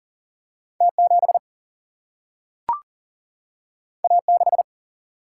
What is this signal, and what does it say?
Morse code, heard as a keyed tone, about 39 wpm, 700 Hz. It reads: T7 A6
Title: Single-Letter Single-Number 40wpm